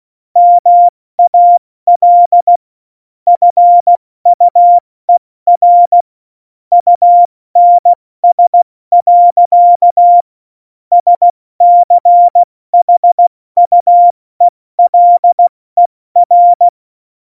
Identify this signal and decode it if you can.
Morse code, heard as a keyed tone, about 16 wpm, 700 Hz. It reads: MAL FUER UNS. SCHUELER
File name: Morsekurs / Witze / Witz 07. Mai 2019, 70 F80